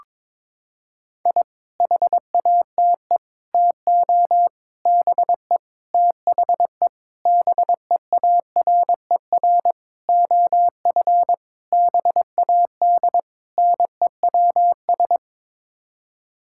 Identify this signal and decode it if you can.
Morse code, heard as a keyed tone, about 22 wpm, 700 Hz. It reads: I HATE TO BE THE BEARER OF BAD NEWS
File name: Sets of 9 Words - 22wpm